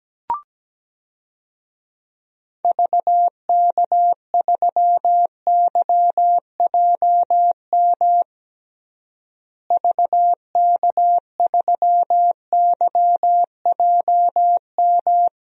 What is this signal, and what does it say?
Morse code, heard as a keyed tone, about 17 wpm, 700 Hz. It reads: VK3YJM VK3YJM
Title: DXCC-100 Call Signs - 6 Characters - Repeated 2x - 17wpm